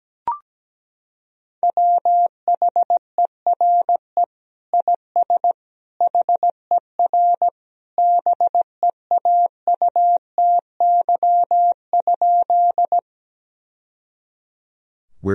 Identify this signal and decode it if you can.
Morse code, heard as a keyed tone, about 17 wpm, 700 Hz. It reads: WHERE IS HER BEAUTY?
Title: Sentences from Top 1000 Words - Review Part 2 - 17wpm